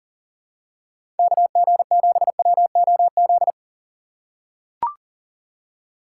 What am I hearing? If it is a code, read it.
Morse code, heard as a keyed tone, about 39 wpm, 700 Hz. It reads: KG7WOZ
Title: USA Call Signs - 6 Characters - 2x Character Spacing - 40wpm